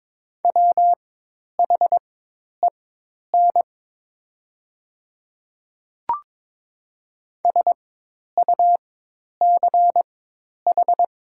Morse code, heard as a keyed tone, about 22 wpm, spaced as for 10 wpm, 700 Hz. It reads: WHEN SUCH
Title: 4x Character Spacing - Top 200 Words - 22wpm